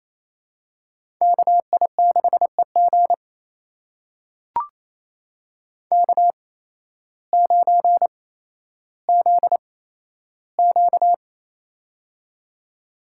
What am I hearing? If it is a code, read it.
Morse code, heard as a keyed tone, about 28 wpm, spaced as for 13 wpm, 700 Hz. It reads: KI6EG K9ZQ
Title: USA Call Signs - 8x Character Spacing - 28wpm